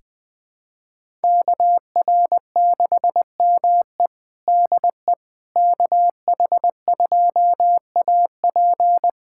Morse code, heard as a keyed tone, about 21 wpm, 700 Hz. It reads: KR6ME DE KH2AP
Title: USA Callsign DE Callsign - 5 Characters - 20wpm